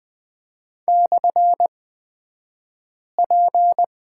Morse code, heard as a keyed tone, about 20 wpm, spaced as for 5 wpm, 700 Hz. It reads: /P
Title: 223 Lesson 36d - BT - Two Characters 20wpm